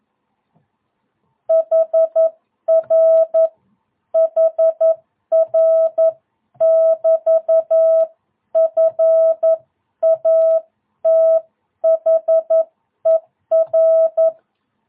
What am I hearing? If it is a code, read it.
Morse code, heard as a keyed tone, about 11 wpm, 650 Hz. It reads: HR HR <BT> FATHER
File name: jh8cbh / 2022cw